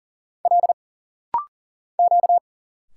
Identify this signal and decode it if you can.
Morse code, heard as a keyed tone, about 42 wpm, 700 Hz. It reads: L Q
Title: Single Letters - Rapid-Fire 40wpm